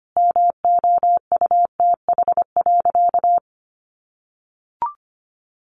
Morse code, heard as a keyed tone, about 25 wpm, 700 Hz. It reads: MOUTH.
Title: Sentences from Top 1500 Words 25wpm